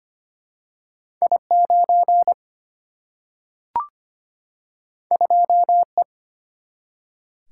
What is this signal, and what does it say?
Morse code, heard as a keyed tone, about 25 wpm, 700 Hz. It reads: I 9 2 E
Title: Single-Letter Single-Number 25wpm